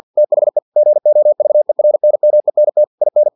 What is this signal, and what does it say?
Morse code, heard as a keyed tone, about 50 wpm, 600 Hz. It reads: THE GOVERNMENT IN
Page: Never 49wpm